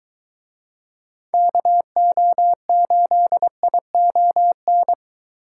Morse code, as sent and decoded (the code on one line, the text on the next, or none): -.- --- ---.. .. --- -.
KO8ION